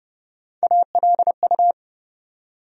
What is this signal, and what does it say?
Morse code, heard as a keyed tone, about 30 wpm, 700 Hz. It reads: ALU
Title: USA Call Sign Postfixes - 3 Letters 30wpm